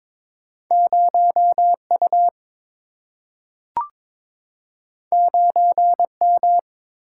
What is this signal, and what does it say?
Morse code, heard as a keyed tone, about 22 wpm, 700 Hz. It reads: 0U 9M
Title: Single-Letter Single-Number 22wpm